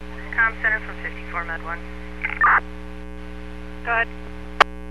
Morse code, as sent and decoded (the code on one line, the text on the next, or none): none